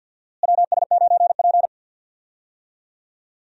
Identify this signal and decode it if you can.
Morse code, heard as a keyed tone, about 48 wpm, 700 Hz. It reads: WS9P
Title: USA Call Signs - 4 Characters - Repeated 2x - 50wpm